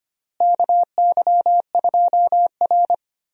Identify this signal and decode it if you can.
Morse code, heard as a keyed tone, about 25 wpm, 700 Hz. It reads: KY2R
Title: USA Call Signs - 4 Characters - Repeated 3x - 25wpm